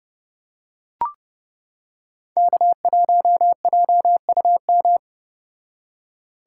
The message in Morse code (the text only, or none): K1JUM